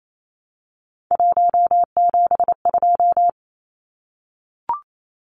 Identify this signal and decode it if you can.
Morse code, heard as a keyed tone, about 28 wpm, 700 Hz. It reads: 172
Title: Numbers - 3-Digit 28wpm